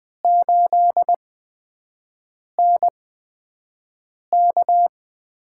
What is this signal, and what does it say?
Morse code, heard as a keyed tone, about 20 wpm, spaced as for 5 wpm, 700 Hz. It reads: 8NK
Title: USA Call Signs - 4 Characters - 8x Character Spacing - 20wpm